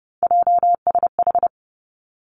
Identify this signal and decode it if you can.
Morse code, heard as a keyed tone, about 32 wpm, 700 Hz. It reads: JSH
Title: Three Challenging Letter-Number Combinations 30wpm